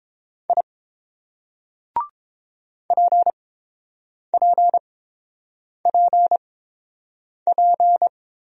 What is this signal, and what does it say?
Morse code, heard as a keyed tone, about 27 wpm, spaced as for 7 wpm, 700 Hz. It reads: I PPPP
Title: Speed-Racing Single Letters 22wpm